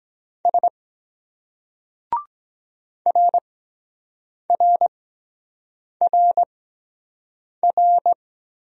Morse code, heard as a keyed tone, about 23 wpm, spaced as for 7 wpm, 700 Hz. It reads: S RRRR